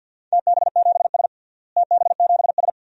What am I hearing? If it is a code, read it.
Morse code, heard as a keyed tone, about 52 wpm, 700 Hz. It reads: TB7S TB7S
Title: DXCC-100 Call Signs - 4 Characters - Repeated 3x - 50wpm